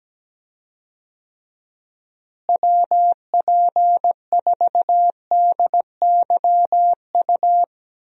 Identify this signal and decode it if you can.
Morse code, heard as a keyed tone, about 17 wpm, 700 Hz. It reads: WP4DYU
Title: USA Call Signs - 6 Characters - Repeated 2x - 17wpm